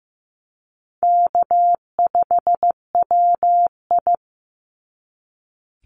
Morse code, heard as a keyed tone, about 15 wpm, 700 Hz. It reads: K5WI
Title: NA SOTA Chaser Callsigns 15wpm